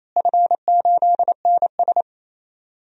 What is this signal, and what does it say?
Morse code, heard as a keyed tone, about 28 wpm, 700 Hz. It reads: F8NS
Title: DXCC-100 Call Signs - 4 Characters - Repeated 2x - 28wpm